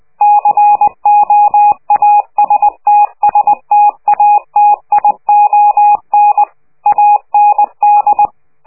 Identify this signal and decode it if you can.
Morse code, heard as a keyed tone, about 20 wpm, 830 Hz. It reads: COAST STATION AND
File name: worldcoastal / scheveningenradio / morsepch